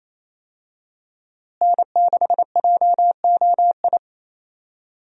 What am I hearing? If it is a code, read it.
Morse code, heard as a keyed tone, about 28 wpm, 700 Hz. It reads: N6JOI
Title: USA Call Signs - 5 Characters - Repeated 2x - 28wpm